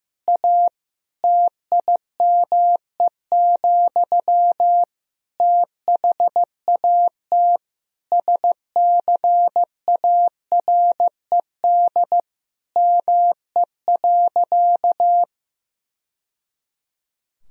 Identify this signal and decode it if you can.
Morse code, heard as a keyed tone, about 15 wpm, 700 Hz. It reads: A TIME, THAT SCARED ME.
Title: Sentences from Top 1000 Words - Review Part 4 - 15wpm